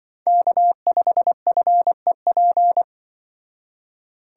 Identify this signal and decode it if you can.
Morse code, heard as a keyed tone, about 24 wpm, 700 Hz. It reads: K5FEP
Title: Speed Racing - USA 1x3 Call Signs 20wpm